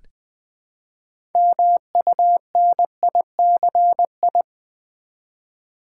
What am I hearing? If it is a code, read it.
Morse code, heard as a keyed tone, about 20 wpm, 700 Hz. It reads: MUNICI
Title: Build A Word - 3 Letters At A Time - 9-Letter Words - 20wpm